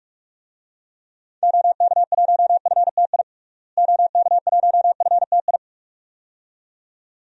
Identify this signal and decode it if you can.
Morse code, heard as a keyed tone, about 45 wpm, 700 Hz. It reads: OK1FTI OK1FTI
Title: DXCC-100 Call Signs - 6 Characters - Repeated 2x - 45wpm